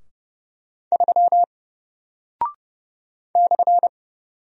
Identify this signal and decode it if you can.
Morse code, heard as a keyed tone, about 30 wpm, 700 Hz. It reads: 3 /
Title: Single Character (Letter:Number:Punctuation:Prosign) Alt 30wpm